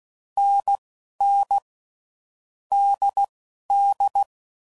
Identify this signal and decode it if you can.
Morse code, heard as a keyed tone, about 16 wpm, 800 Hz. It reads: N N D D
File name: Combinations / Difficult Combos / NDB NDB6